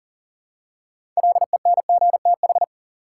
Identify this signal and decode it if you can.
Morse code, heard as a keyed tone, about 40 wpm, 700 Hz. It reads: LENGTH
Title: Six Letter Words - 40wpm